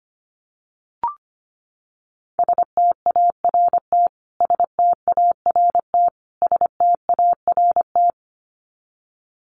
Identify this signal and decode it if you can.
Morse code, heard as a keyed tone, about 25 wpm, 700 Hz. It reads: START START START
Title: Top 500 Words Repeated 25wpm